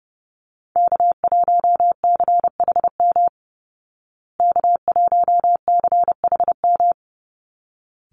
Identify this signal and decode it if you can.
Morse code, heard as a keyed tone, about 30 wpm, 700 Hz. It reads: K1CHM K1CHM
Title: Call Signs Repeated 3x - 30wpm